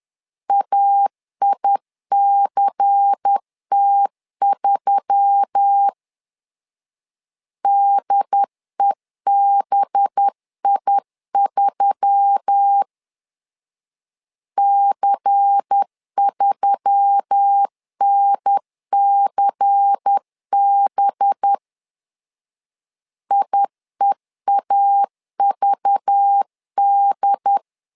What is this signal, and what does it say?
Morse code, heard as a keyed tone, about 10 wpm, 800 Hz. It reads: AICT3 DEBI3 C3NCB IEAVD
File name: LEZIONI / 5 LBVC / 5 LBVC 10wpm 2349